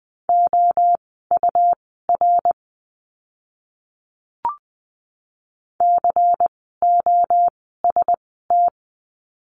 Morse code, heard as a keyed tone, about 20 wpm, spaced as for 14 wpm, 700 Hz. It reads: OUR COST
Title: 101 Lesson 17f - Letter u - Words 20wpm